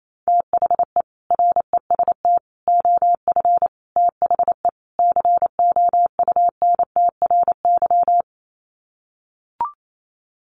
Morse code, heard as a keyed tone, about 28 wpm, 700 Hz. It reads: THE REST OF THE COUNTRY
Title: Sets of 5 Words 28wpm